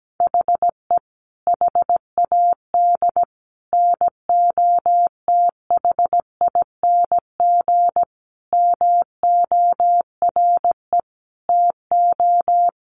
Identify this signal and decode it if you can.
Morse code, heard as a keyed tone, about 17 wpm, 700 Hz. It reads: HE HAD NOTHING MORE TO
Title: Sentences from Top 1100 Words 17wpm